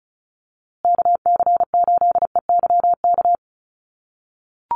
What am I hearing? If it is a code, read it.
Morse code, heard as a keyed tone, about 34 wpm, 700 Hz. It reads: KC8EYK
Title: USA Call Signs Repeated 2x 35wpm